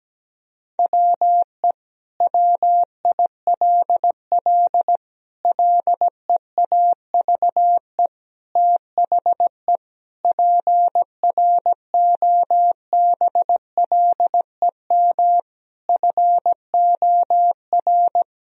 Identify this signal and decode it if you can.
Morse code, heard as a keyed tone, about 17 wpm, 700 Hz. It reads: WE WILL LEAVE THE PROBLEM FOR
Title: Sentences from Top 1000 Words - Review Part 2 - 17wpm